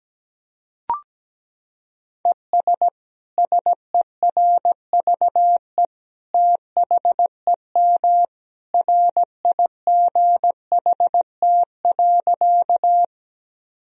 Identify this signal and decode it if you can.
Morse code, heard as a keyed tone, about 17 wpm, 700 Hz. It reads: ES SERVE THEM RIGHT.
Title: Sentences from Top 400 Words and Abbreviations 17wpm